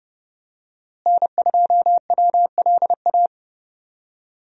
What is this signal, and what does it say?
Morse code, heard as a keyed tone, about 31 wpm, 700 Hz. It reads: N2WLA
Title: USA Call Signs - 5 Characters - Repeated 2x - 30wpm